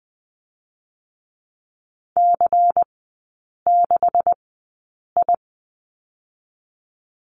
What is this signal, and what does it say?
Morse code, heard as a keyed tone, about 20 wpm, spaced as for 8 wpm, 700 Hz. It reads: C6I
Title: 138 Lesson 23e - Letter f - Three Characters 20wpm